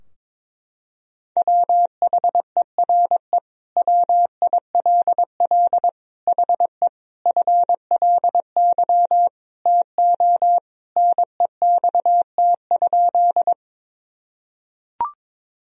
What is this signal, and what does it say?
Morse code, heard as a keyed tone, about 22 wpm, 700 Hz. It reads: WHERE WILL HE FLY TO NEXT?
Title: Sentences from Top 1000 Words - Review Part 2 - 22wpm